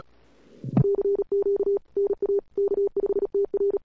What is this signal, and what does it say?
Morse code, heard as a keyed tone, about 36 wpm, 410 Hz. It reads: CQ NA K5TR
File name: cw-sprint-k5tr-op-n5rz